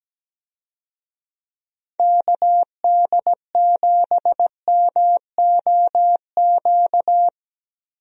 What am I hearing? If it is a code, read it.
Morse code, heard as a keyed tone, about 17 wpm, 700 Hz. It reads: KD7MOQ
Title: USA Call Signs - 6 Characters - Repeated 3x - 17wpm